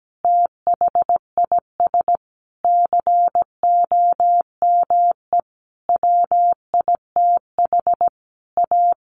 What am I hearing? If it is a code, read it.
Morse code, heard as a keyed tone, about 17 wpm, 700 Hz. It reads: THIS COME WITH A
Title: Sentences from Top 1400 Words 17wpm